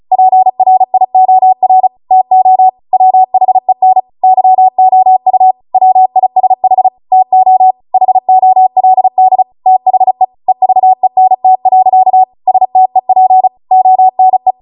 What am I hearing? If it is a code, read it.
Morse code, heard as a keyed tone, about 35 wpm, 750 Hz. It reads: PRIOR TO WHEN YOU WISH TO HOLD THE EVENT. STEP ONE
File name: Morse / 35WPM